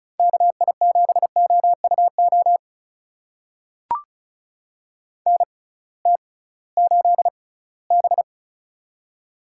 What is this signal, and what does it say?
Morse code, heard as a keyed tone, about 36 wpm, spaced as for 21 wpm, 700 Hz. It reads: KI7OUO NT8B
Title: USA Call Signs - 6x Character Spacing - 35wpm